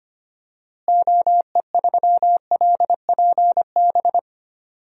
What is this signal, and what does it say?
Morse code, heard as a keyed tone, about 25 wpm, 700 Hz. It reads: OE3LPB
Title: DXCC-100 Call Signs - 6 Characters - Repeated 2x - 25wpm